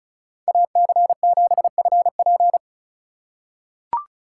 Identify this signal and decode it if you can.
Morse code, heard as a keyed tone, about 36 wpm, 700 Hz. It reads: AC7FP
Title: USA Call Signs - 5 Characters - 4x Character Spacing - 35wpm